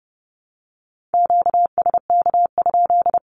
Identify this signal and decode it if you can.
Morse code, heard as a keyed tone, about 30 wpm, 700 Hz. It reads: QSK?
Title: Q-Codes with Spoken Meaning - 30wpm